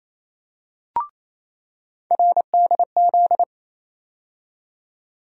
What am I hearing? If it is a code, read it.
Morse code, heard as a keyed tone, about 28 wpm, 700 Hz. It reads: RDZ
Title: USA Call Sign Postfixes - 3 Letters 28wpm